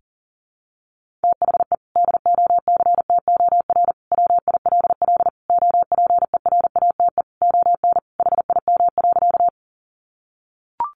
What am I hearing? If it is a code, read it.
Morse code, heard as a keyed tone, about 40 wpm, 700 Hz. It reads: THE DOCTOR WILL OPERATE ON HIM.
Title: Sentences from Top 1500 Words 40wpm